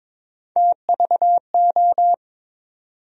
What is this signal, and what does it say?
Morse code, heard as a keyed tone, about 22 wpm, 700 Hz. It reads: TVO
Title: USA Call Sign Postfixes - 3 Letters 22wpm